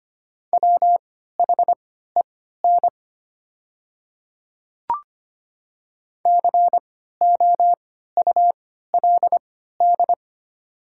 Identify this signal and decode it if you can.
Morse code, heard as a keyed tone, about 25 wpm, spaced as for 14 wpm, 700 Hz. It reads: WHEN COULD
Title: 3x Character Spacing - Top 100 Words - 25wpm